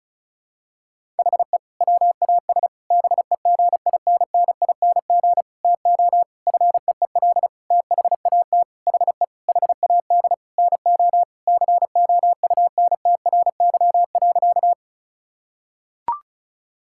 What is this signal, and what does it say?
Morse code, heard as a keyed tone, about 36 wpm, 700 Hz. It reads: HE WAS BEGINNING TO FEEL THAT HE HAD NO COUNTRY.
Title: Sentences from Top 1000 Words - Review Part 5 - 35wpm